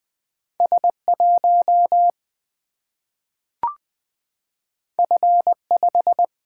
Morse code, heard as a keyed tone, about 20 wpm, 700 Hz. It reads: S 1 F 5
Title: Single-Letter Single-Number 20wpm